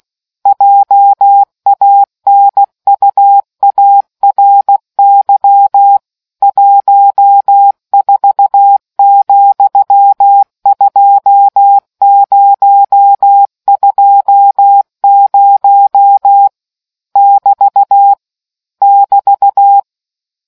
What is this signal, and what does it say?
Morse code, heard as a keyed tone, about 16 wpm, 800 Hz. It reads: JANUARY 14,2020 <BT> <BT>